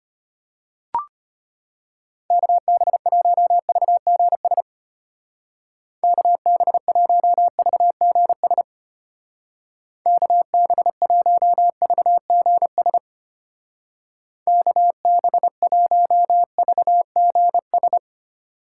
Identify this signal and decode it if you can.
Morse code, heard as a keyed tone, about 32 wpm, 700 Hz. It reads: KB1VGS KB1VGS KB1VGS KB1VGS